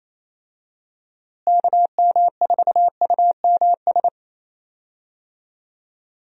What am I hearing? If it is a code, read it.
Morse code, heard as a keyed tone, about 28 wpm, 700 Hz. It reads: KM4UMS
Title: USA Call Signs - 6 Characters - Repeated 2x - 28wpm